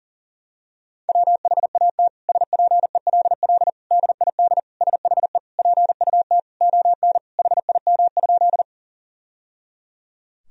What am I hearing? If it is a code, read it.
Morse code, heard as a keyed tone, about 40 wpm, 700 Hz. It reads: WHAT SPELL DID SHE PUT ON HIM?
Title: Sentences from Top 1000 Words - Review Part 1 - 40wpm